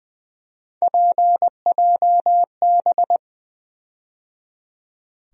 Morse code, heard as a keyed tone, about 20 wpm, 700 Hz. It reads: PJB
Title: USA Call Sign Postfixes - 3 Letters 20wpm